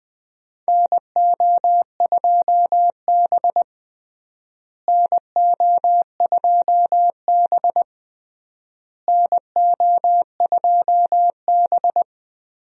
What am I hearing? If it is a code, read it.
Morse code, heard as a keyed tone, about 20 wpm, 700 Hz. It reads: NO2B NO2B NO2B
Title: USA Call Signs - 4 Characters - Repeated 3x - 20wpm